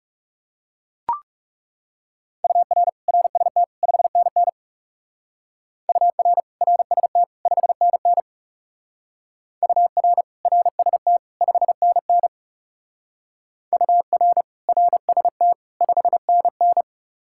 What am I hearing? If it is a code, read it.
Morse code, heard as a keyed tone, about 38 wpm, 700 Hz. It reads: UR RST 5NN UR RST 5NN UR RST 5NN UR RST 5NN